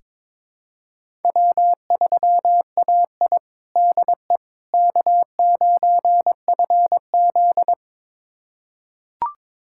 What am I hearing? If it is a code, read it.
Morse code, heard as a keyed tone, about 22 wpm, 700 Hz. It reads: W3AI DE K9FZ
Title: USA Callsign DE Callsign - 4 Characters - 22wpm